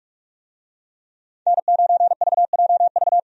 Morse code, heard as a keyed tone, about 46 wpm, 700 Hz. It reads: N9VJV
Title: USA Call Signs - 5 Characters - 45wpm